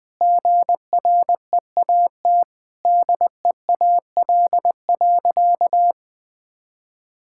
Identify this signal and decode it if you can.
Morse code, heard as a keyed tone, about 20 wpm, 700 Hz. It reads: GREAT DEAL.
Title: Sentences with 3-Word Phrases - 20wpm